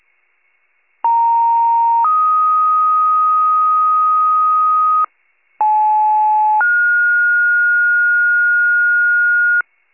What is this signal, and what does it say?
Two-tone page, heard as a one-second tone, then a three-second tone, about 930 then 1250 Hz, about 860 then 1430 Hz, with noise. It is heard from 1.0 to 9.6 s.